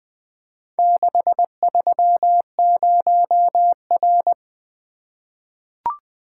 Morse code, heard as a keyed tone, about 21 wpm, 700 Hz. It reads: 630R
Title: Four Letter-Number Combinations - 20wpm